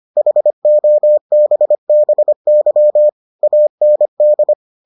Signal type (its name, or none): Morse code